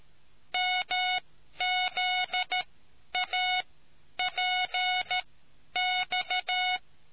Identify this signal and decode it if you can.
Morse code, heard as a keyed tone, about 13 wpm, spaced as for 10 wpm, 730 Hz. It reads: MZAPX